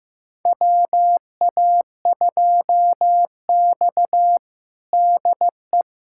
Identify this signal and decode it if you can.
Morse code, heard as a keyed tone, about 15 wpm, 700 Hz. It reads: WA2X DE
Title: USA Callsign DE Callsign - 4 Characters - 15wpm